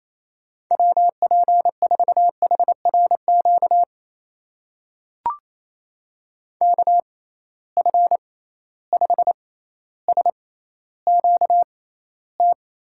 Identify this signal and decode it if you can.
Morse code, heard as a keyed tone, about 28 wpm, spaced as for 14 wpm, 700 Hz. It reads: WP4HRQ KF5SQT
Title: USA Call Signs - 6 Characters - 6x Character Spacing - 28wpm